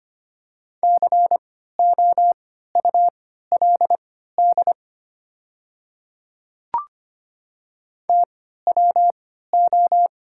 Morse code, heard as a keyed tone, about 25 wpm, spaced as for 14 wpm, 700 Hz. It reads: COULD TWO
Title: 3x Character Spacing - Top 200 Words - 25wpm